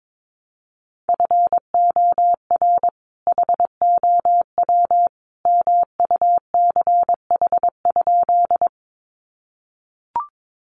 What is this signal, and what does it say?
Morse code, heard as a keyed tone, about 22 wpm, 700 Hz. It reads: FOR HOW MUCH?
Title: Sentences from Top 200 Words 22wpm